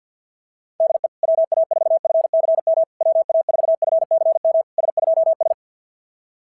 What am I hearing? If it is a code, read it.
Morse code, heard as a keyed tone, about 50 wpm, 650 Hz. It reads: DE WA4FCM WA4FCM S2S
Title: SOTA QSO Practice 50wpm